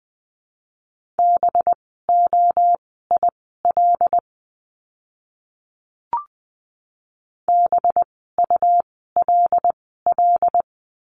Morse code, heard as a keyed tone, about 20 wpm, spaced as for 14 wpm, 700 Hz. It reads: BOIL BULL
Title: 185 Lesson 30f - Letter b - Words 20wpm